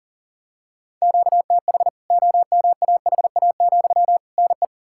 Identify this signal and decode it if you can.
Morse code, heard as a keyed tone, about 37 wpm, 700 Hz. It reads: QTH OMAHA, NE
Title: K1USN SST QSO Practice 40wpm